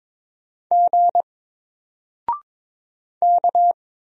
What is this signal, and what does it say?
Morse code, heard as a keyed tone, about 22 wpm, 700 Hz. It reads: G K
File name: Single-Letter-Number-22wpm